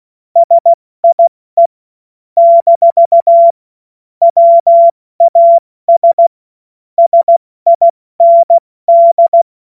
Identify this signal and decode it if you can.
Morse code, heard as a keyed tone, about 16 wpm, 680 Hz. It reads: SIE - WAS SIND
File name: Morsekurs / Witze / Witz 06. Januar 2026 70 F80